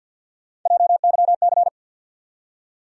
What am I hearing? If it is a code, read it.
Morse code, heard as a keyed tone, about 53 wpm, 700 Hz. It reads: JYC
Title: USA Call Sign Postfixes - 3 Letters 50wpm